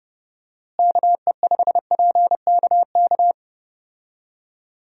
Morse code, heard as a keyed tone, about 30 wpm, 700 Hz. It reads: KE5PKK
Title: USA Call Signs - 6 Characters - Repeated 2x - 30wpm